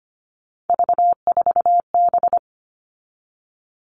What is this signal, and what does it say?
Morse code, heard as a keyed tone, about 25 wpm, 700 Hz. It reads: V4B
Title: All Cut Numbers - 3-Digit 25wpm